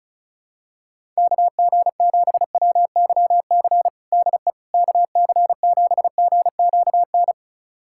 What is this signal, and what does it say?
Morse code, heard as a keyed tone, about 34 wpm, 700 Hz. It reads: KG7WYC DE KC7GQN
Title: USA Callsign DE Callsign 35wpm